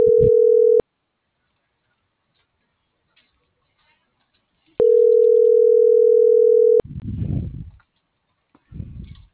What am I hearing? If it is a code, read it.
Ringback tone (North America), 440 + 480 Hz.